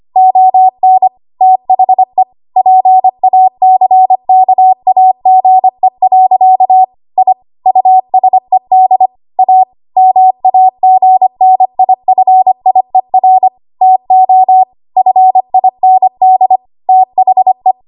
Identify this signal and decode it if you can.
Morse code, heard as a keyed tone, about 25 wpm, 750 Hz. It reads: ON THE PACKAGE. I USED A MAGNIFIER TO FIND THE
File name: Archive / 25 WPM / 25WPM